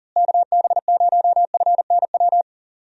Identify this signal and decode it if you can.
Morse code, heard as a keyed tone, about 40 wpm, 700 Hz. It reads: KB0FNW
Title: USA Call Signs - 6 Characters - Repeated 2x - 40wpm